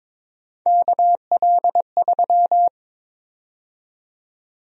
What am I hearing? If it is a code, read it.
Morse code, heard as a keyed tone, about 22 wpm, 700 Hz. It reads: KL3
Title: USA Call Sign Prefixes - 22wpm